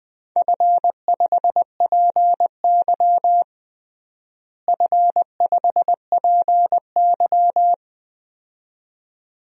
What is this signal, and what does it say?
Morse code, heard as a keyed tone, about 21 wpm, 700 Hz. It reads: F5PY F5PY
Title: DXCC-100 Call Signs - 4 Characters - Repeated 2x - 20wpm